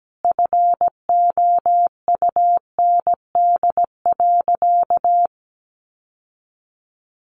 Morse code, heard as a keyed tone, about 17 wpm, 700 Hz. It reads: FOUND.
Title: Sentences from Top 1300 Words 17wpm